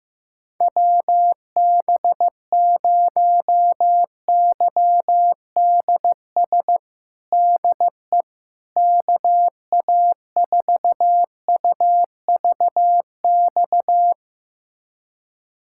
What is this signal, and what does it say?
Morse code, heard as a keyed tone, about 15 wpm, 700 Hz. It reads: WB0YDS DE KA4UVX
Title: USA Callsign DE Callsign - 6 Characters - 15wpm